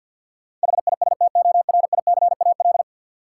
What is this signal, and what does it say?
Morse code, heard as a keyed tone, about 50 wpm, 700 Hz. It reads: HISTORICAL
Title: Sentences from Top 600 Words - New Word Drill - 50wpm